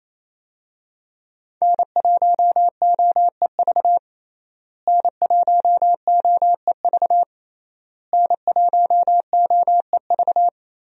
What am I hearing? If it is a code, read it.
Morse code, heard as a keyed tone, about 28 wpm, 700 Hz. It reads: N1OEV N1OEV N1OEV
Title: USA Call Signs - 5 Characters - Repeated 3x - 28wpm